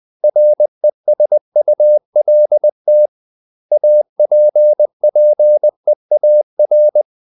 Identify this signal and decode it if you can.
Morse code, heard as a keyed tone, about 20 wpm, 600 Hz. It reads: RESULT APPEAR
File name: CW / Practice / 500MostCommonEnglishWords20